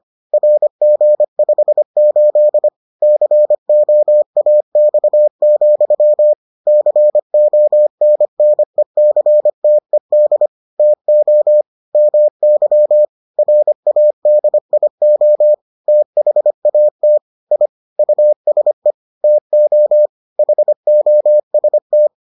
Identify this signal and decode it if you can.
Morse code, heard as a keyed tone, about 25 wpm, 600 Hz. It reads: RG58 COAX, CONNECTED TO MY RADIO THAT I USE TO HOST